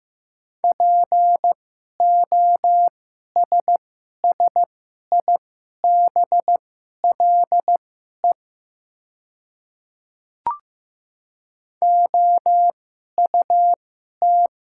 Morse code, heard as a keyed tone, about 15 wpm, spaced as for 11 wpm, 690 Hz. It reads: POSSIBLE OUT
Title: 2x Character Spacing - Top 500 Words - 15wpm